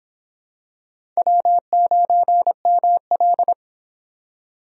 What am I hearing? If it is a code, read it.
Morse code, heard as a keyed tone, about 23 wpm, 700 Hz. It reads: W9ML